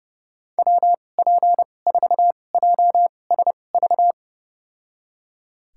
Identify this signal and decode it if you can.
Morse code, heard as a keyed tone, about 31 wpm, spaced as for 22 wpm, 700 Hz. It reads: WP4JSV
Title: USA Call Signs - 6 Characters - 2x Character Spacing - 30wpm